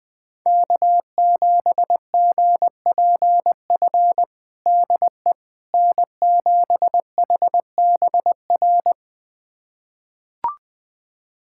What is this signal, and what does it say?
Morse code, heard as a keyed tone, about 20 wpm, 700 Hz. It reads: K7GPF DE N7HBR
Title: USA Callsign DE Callsign - 5 Characters - 20wpm